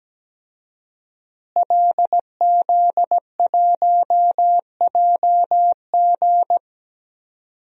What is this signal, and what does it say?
Morse code, heard as a keyed tone, about 17 wpm, 700 Hz. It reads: LZ1JG
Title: DXCC-100 Call Signs - 5 Characters - Repeated 2x - 17wpm